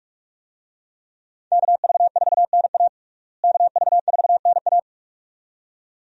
Morse code, heard as a keyed tone, about 46 wpm, 700 Hz. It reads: KV4NA KV4NA
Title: USA Call Signs - 5 Characters - Repeated 2x - 45wpm